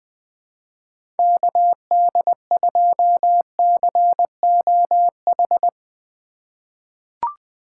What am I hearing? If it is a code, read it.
Morse code, heard as a keyed tone, about 20 wpm, 700 Hz. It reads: KD2COH
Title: USA Call Signs - 6 Characters - 3x Character Spacing - 20wpm